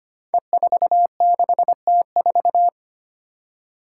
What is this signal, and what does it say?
Morse code, heard as a keyed tone, about 25 wpm, 700 Hz. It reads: E46T4
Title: Five Challenging Letter-Number Combinations 25wpm